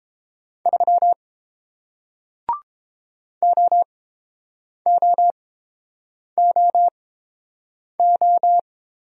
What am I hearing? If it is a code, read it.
Morse code, heard as a keyed tone, about 30 wpm, spaced as for 7 wpm, 700 Hz. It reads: 3 OOOO